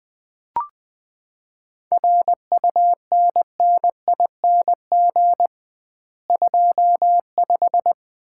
Morse code, heard as a keyed tone, about 20 wpm, 700 Hz. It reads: RUNNING 25
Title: Common QSO Phrases - 2x Word-Spacing - 20wpm